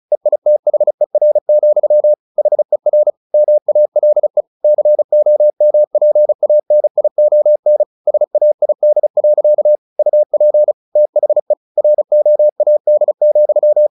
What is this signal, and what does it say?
Morse code, heard as a keyed tone, about 34 wpm, 600 Hz. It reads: EITHER, HER MALE COMPANION SAID. UP THE ROAD,
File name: the old man and the sea 2 35wpm